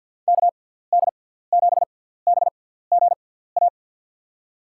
Morse code, heard as a keyed tone, about 53 wpm, spaced as for 17 wpm, 700 Hz. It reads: KD7BGA
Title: USA Call Signs - 6 Characters - 6x Character Spacing - 50wpm